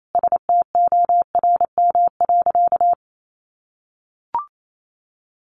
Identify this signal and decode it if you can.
Morse code, heard as a keyed tone, about 28 wpm, 700 Hz. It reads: STORM.
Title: Sentences from Top 1500 Words 28wpm